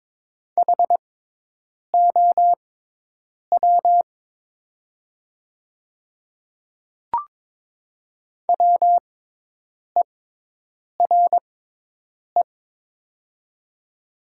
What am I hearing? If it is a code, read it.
Morse code, heard as a keyed tone, about 21 wpm, spaced as for 8 wpm, 700 Hz. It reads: HOW WERE
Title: 6x Character Spacing - Top 300 Words - 22wpm